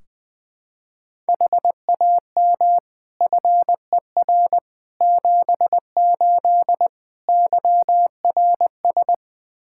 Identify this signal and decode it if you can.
Morse code, heard as a keyed tone, about 20 wpm, 700 Hz. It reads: HAM FER 78 YRS
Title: Practice QSO - Exchange 3 - More Info - 20wpm